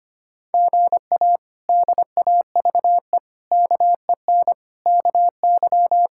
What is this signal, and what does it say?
Morse code, heard as a keyed tone, about 25 wpm, 700 Hz. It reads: GA DAVE KEN KY
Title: K1USN SST QSO Practice - 25wpm